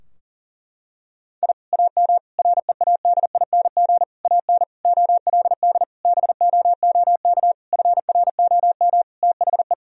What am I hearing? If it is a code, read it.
Morse code, heard as a keyed tone, about 40 wpm, 700 Hz. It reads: I AM READING AN OLD BOOK FROM THE
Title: Sentences from Top 1000 Words - Review Part 4 - 40wpm